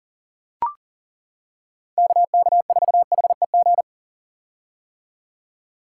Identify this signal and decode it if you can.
Morse code, heard as a keyed tone, about 40 wpm, 700 Hz. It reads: KK4HEG
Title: K1USN SST QSO Practice - 40wpm